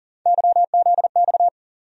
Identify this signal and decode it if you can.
Morse code, heard as a keyed tone, about 40 wpm, 700 Hz. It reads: YZX
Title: USA Call Sign Postfixes - 3 Letters 40wpm